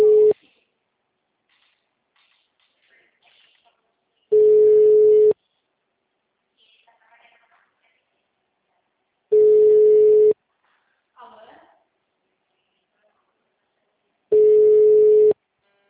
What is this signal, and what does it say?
Ringback tone (Europe, 425 Hz), with speech.
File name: out- -2002- - - .6923